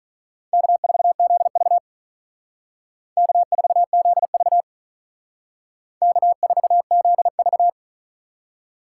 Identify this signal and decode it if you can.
Morse code, heard as a keyed tone, about 44 wpm, 700 Hz. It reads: K4ZV K4ZV K4ZV